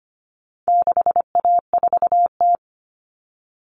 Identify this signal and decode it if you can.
Morse code, heard as a keyed tone, about 25 wpm, 700 Hz. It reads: 6A4T
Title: Common Cut Numbers - 4-Digit 25wpm